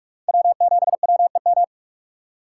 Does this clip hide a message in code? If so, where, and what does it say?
Yes: Morse code from 0.3 to 1.6 s, heard as a keyed tone, about 45 wpm, 700 Hz. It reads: W7WEM